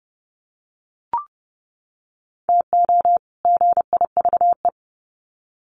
Morse code, heard as a keyed tone, about 30 wpm, 700 Hz. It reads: TO GIVE
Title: Sets of 2 Words 30wpm